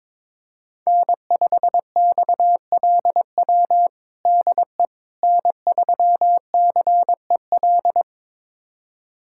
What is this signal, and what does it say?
Morse code, heard as a keyed tone, about 22 wpm, 700 Hz. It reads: N5XLW DE N3CEL
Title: USA Callsign DE Callsign 22wpm